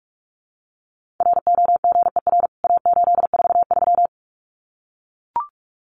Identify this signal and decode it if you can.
Morse code, heard as a keyed tone, about 44 wpm, 700 Hz. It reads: ROGER A843
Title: CWT Exchange 45wpm